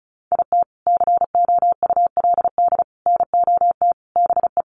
Morse code, heard as a keyed tone, about 34 wpm, 700 Hz. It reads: IT COULD NOT BE
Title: Sentences from Top 1200 Words 35wpm